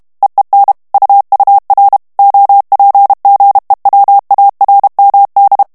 Morse code, heard as a keyed tone, about 32 wpm, 800 Hz. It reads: EEN UUR OPGEWARMD
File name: HOLLANDAIS texte clair 2 vitesse 30 mots minute